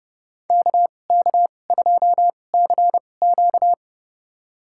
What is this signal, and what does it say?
Morse code, heard as a keyed tone, about 30 wpm, spaced as for 22 wpm, 700 Hz. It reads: KK2CQ
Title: USA Call Signs - 5 Characters - 2x Character Spacing - 30wpm